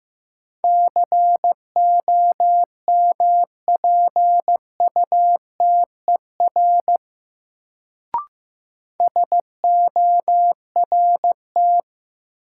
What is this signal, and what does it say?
Morse code, heard as a keyed tone, about 15 wpm, 700 Hz. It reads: COMPUTER SORT
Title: Sentences from Top 1000 Words - Review Part 3 - New Word Drill - 15wpm